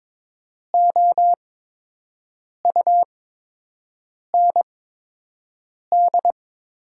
Morse code, heard as a keyed tone, about 22 wpm, spaced as for 6 wpm, 700 Hz. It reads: OUND
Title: 8x Character Spacing - Top 300 Words - 22wpm